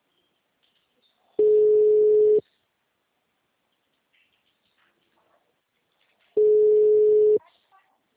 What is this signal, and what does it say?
Ringback tone (Europe, 425 Hz), with speech.